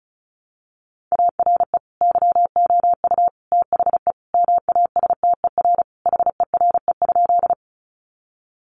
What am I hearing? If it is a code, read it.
Morse code, heard as a keyed tone, about 34 wpm, 700 Hz. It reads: ARE YOU THE MASTER HERE?
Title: Sentences from Top 1300 Words 35wpm